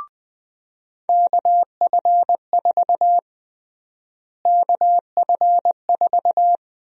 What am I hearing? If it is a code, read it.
Morse code, heard as a keyed tone, about 21 wpm, 700 Hz. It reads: KF4 KF4
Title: USA Call Sign Prefixes - Repeated 2x - 20wpm